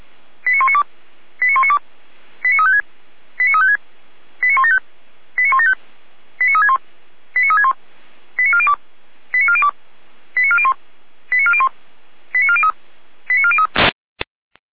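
Five-tone selective call (ZVEI), heard as quick runs of five tones, tones of about 80 ms. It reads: 89192 89192 89356 89356 89175 89175 89361 89361 89402 89402 89401 89401 89403 89403
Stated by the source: ZVEI five-tone selective call received by SDR.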